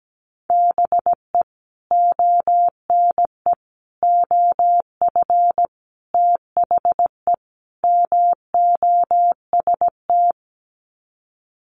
Morse code, heard as a keyed tone, about 17 wpm, 690 Hz. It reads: BE ONE OF THE MOST
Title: Sets of 5 Words Encore 17wpm